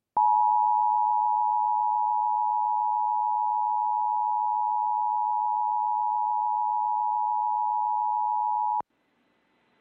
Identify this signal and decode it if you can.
Emergency Alert System (SAME) tones (the 853 and 960 Hz attention tone). It is heard from 0.6 to 8.9 s.